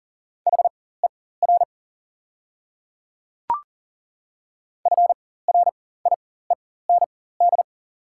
Morse code, heard as a keyed tone, about 40 wpm, spaced as for 19 wpm, 690 Hz. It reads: HER FRIEND
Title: 4x Character Spacing - Top 300 Words - 40wpm